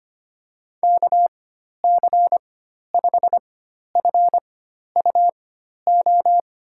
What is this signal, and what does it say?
Morse code, heard as a keyed tone, about 25 wpm, spaced as for 12 wpm, 700 Hz. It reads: KC5FUO
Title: USA Call Signs - 4x Character Spacing - 25wpm